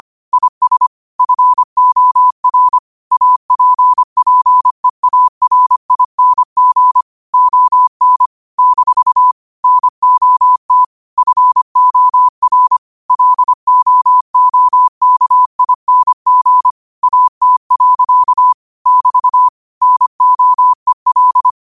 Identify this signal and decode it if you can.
Morse code, heard as a keyed tone, about 25 wpm, 1000 Hz. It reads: IS FOR APPEARING ON - NOT FOR LOOKING AT. <BT> NOEL
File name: qotdfiles / 25wpm